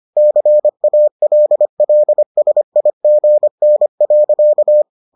Morse code, heard as a keyed tone, about 25 wpm, 600 Hz. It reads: CALLSIGN.